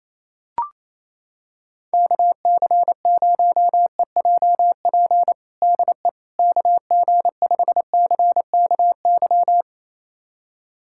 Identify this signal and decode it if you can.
Morse code, heard as a keyed tone, about 28 wpm, 700 Hz. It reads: KC0EJP DE KG5CKY
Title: USA Callsign DE Callsign - 6 Characters - 28wpm